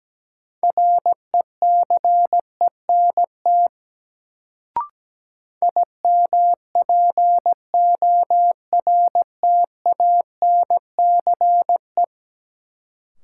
Morse code, heard as a keyed tone, about 17 wpm, 700 Hz. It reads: RECENT IMPORTANCE
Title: Sets of 3 Words - New Word Drill - 17wpm